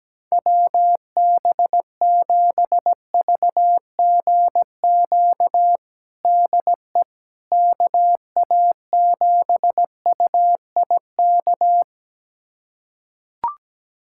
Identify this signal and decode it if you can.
Morse code, heard as a keyed tone, about 17 wpm, 700 Hz. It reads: WB7VGQ DE KA7UIK
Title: USA Callsign DE Callsign - 6 Characters - 17wpm